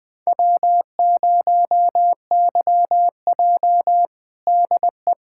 Morse code, heard as a keyed tone, about 20 wpm, 700 Hz. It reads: W0YJ DE
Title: USA Callsign DE Callsign - 4 Characters - 20wpm